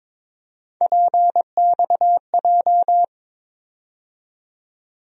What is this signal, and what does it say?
Morse code, heard as a keyed tone, about 22 wpm, 700 Hz. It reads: PXJ
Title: USA Call Sign Postfixes - 3 Letters 22wpm